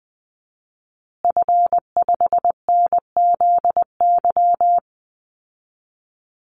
Morse code, heard as a keyed tone, about 20 wpm, 700 Hz. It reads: F5NZY
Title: Call Signs Repeated 2x - 20wpm